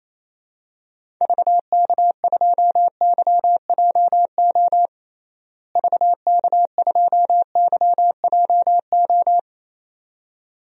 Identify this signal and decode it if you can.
Morse code, heard as a keyed tone, about 28 wpm, 700 Hz. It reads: VK2YJO VK2YJO
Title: DXCC-100 Call Signs - 6 Characters - Repeated 2x - 28wpm